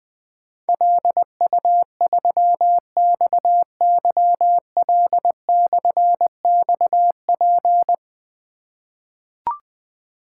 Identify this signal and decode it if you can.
Morse code, heard as a keyed tone, about 20 wpm, 700 Hz. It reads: LU3XYL/XP
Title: Difficult Call Signs 20wpm